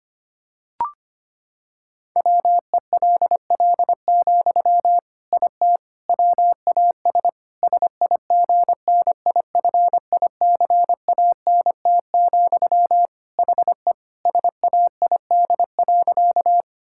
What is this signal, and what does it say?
Morse code, heard as a keyed tone, about 25 wpm, 700 Hz. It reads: WELL, IT WAS SIGNIFICANT, HE SAID.
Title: Sentences from Top 1000 Words - Review Part 6 - 25wpm